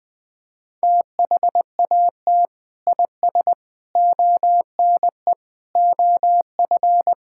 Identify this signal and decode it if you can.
Morse code, heard as a keyed tone, about 20 wpm, 700 Hz. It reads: THAT IS ONE OF
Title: Sentences with 2-Word Phrases - 20wpm